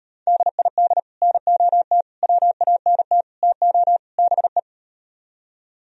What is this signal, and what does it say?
Morse code, heard as a keyed tone, about 38 wpm, 700 Hz. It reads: DID NOT WANT TO BE